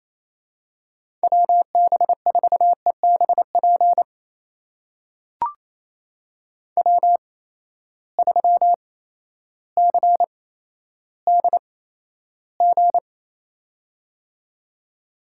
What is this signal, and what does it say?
Morse code, heard as a keyed tone, about 28 wpm, spaced as for 13 wpm, 700 Hz. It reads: WB4EBP W3CDG
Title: USA Call Signs - 8x Character Spacing - 28wpm